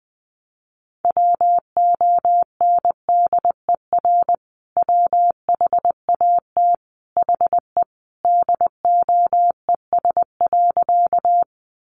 Morse code, heard as a keyed tone, about 20 wpm, 700 Hz. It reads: WONDER WHAT HE DOES.
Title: Sentences from Top 500 Words 20wpm